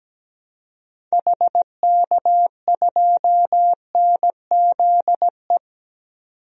Morse code, heard as a keyed tone, about 17 wpm, 700 Hz. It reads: HK2NZE
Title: DXCC-100 Call Signs - 6 Characters - 17wpm